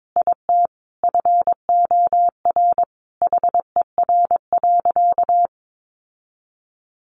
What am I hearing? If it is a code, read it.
Morse code, heard as a keyed tone, about 22 wpm, 700 Hz. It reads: IT FOR HER.
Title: Sentences from Top 1500 Words 22wpm